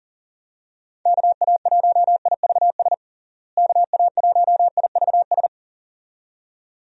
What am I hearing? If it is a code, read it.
Morse code, heard as a keyed tone, about 40 wpm, 700 Hz. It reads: KA1IVS KA1IVS
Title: USA Call Signs - 6 Characters - Repeated 2x - 40wpm